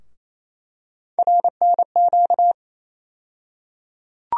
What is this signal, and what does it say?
Morse code, heard as a keyed tone, about 27 wpm, 700 Hz. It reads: RNQ
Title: USA Call Sign Postfixes - 3 Letters 28wpm